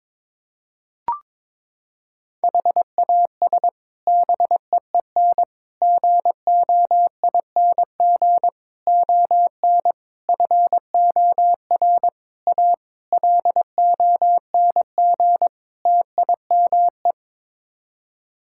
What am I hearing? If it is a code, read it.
Morse code, heard as a keyed tone, about 22 wpm, 700 Hz. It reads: HAS BEEN GOING ON FOR A LONG TIME
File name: practice / 22wpm / Sets of 8 Words - 22wpm